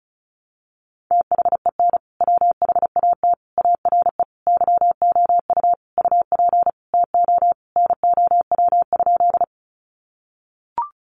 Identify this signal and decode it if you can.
Morse code, heard as a keyed tone, about 34 wpm, 700 Hz. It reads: THEN WHAT ARE YOU UP TO NOW?
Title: Sentences from Top 100 Words 35wpm